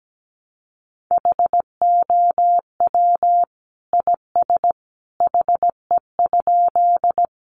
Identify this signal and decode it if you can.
Morse code, heard as a keyed tone, about 17 wpm, 700 Hz. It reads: HOW IS HE?
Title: Sentences from Top 100 Words 17wpm